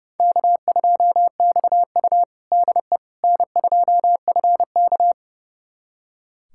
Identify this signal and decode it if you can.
Morse code, heard as a keyed tone, about 30 wpm, 700 Hz. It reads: K2XU DE N2FK
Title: USA Callsign DE Callsign - 4 Characters - 30wpm